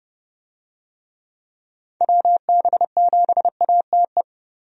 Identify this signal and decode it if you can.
Morse code, heard as a keyed tone, about 30 wpm, 700 Hz. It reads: WB7ATE
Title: USA Call Signs - 6 Characters - Repeated 2x - 30wpm